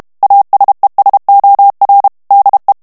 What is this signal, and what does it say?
Morse code, heard as a keyed tone, about 32 wpm, 800 Hz. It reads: ASESOR DE
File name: ESPAGNOL TEXTE EN CLAIR 2 VITESSE 30 MOTS MINUTES